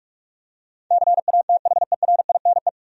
Morse code, heard as a keyed tone, about 46 wpm, 700 Hz. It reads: CATHERINE
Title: USA Common Names - 45wpm